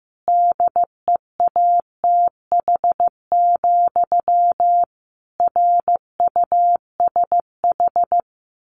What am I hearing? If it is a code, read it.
Morse code, heard as a keyed tone, about 15 wpm, 700 Hz. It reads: DEATH, RUSH
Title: A Sound Of Thunder 15wpm